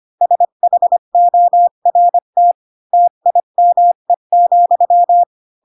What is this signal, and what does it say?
Morse code, heard as a keyed tone, about 25 wpm, 700 Hz. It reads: SHORT TIME,
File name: Pinocchio / Pinocchio ch4 20 25 ik1hge